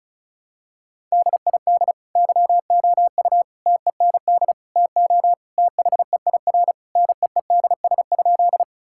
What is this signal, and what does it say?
Morse code, heard as a keyed tone, about 36 wpm, 700 Hz. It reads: DID YOU TEND TO THEIR NEEDS?
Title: Sentences from Top 1000 Words - Review Part 5 - 35wpm